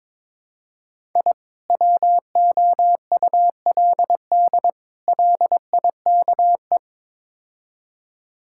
Morse code, heard as a keyed tone, about 22 wpm, 700 Hz. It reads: I WOULD LIKE
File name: practice / 22wpm / Build A Phrase - 5-Word Phrases - 22wpm